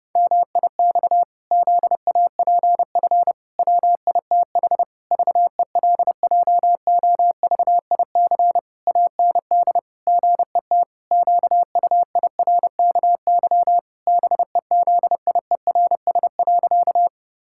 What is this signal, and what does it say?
Morse code, heard as a keyed tone, about 30 wpm, 700 Hz. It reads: MIX ZAPF WITH VELJOVIC AND GET QUIRKY BEZIERS.
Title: Pangram Challenge v2 - 30wpm